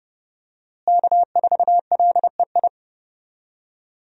Morse code, heard as a keyed tone, about 32 wpm, 700 Hz. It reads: K4LEI
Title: USA Call Signs - 5 Characters - Repeated 3x - 30wpm